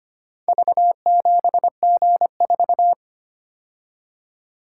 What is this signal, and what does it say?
Morse code, heard as a keyed tone, about 25 wpm, 700 Hz. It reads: V7G4
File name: practice / 25wpm / Four Letter-Number Combinations - 25wpm